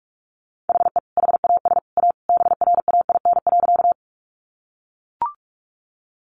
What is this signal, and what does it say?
Morse code, heard as a keyed tone, about 44 wpm, 700 Hz. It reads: HE HAS A BRAIN.
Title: Sentences from Top 1100 Words 45wpm